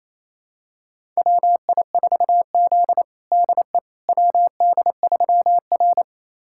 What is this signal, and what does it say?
Morse code, heard as a keyed tone, about 28 wpm, 700 Hz. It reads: WI4Z DE WD3R
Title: USA Callsign DE Callsign - 4 Characters - 28wpm